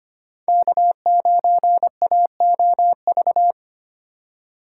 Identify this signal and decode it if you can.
Morse code, heard as a keyed tone, about 25 wpm, 700 Hz. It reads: K9AOV
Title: USA Call Signs - 4x Character Spacing - 25wpm